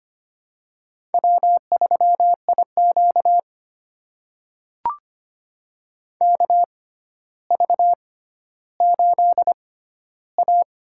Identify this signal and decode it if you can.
Morse code, heard as a keyed tone, about 25 wpm, spaced as for 13 wpm, 700 Hz. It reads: W3IQ KV8A
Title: USA Call Signs - 4 Characters - 6x Character Spacing - 25wpm